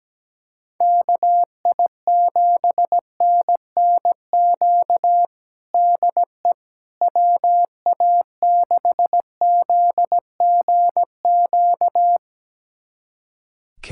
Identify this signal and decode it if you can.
Morse code, heard as a keyed tone, about 17 wpm, 700 Hz. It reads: KI7NNQ DE WA6ZGQ
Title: USA Callsign DE Callsign - 6 Characters - 17wpm